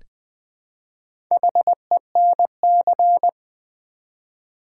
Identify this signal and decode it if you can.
Morse code, heard as a keyed tone, about 20 wpm, 700 Hz. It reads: HENC
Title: Build A Word - 1 Letter At A Time - 5-Letter Words - 20wpm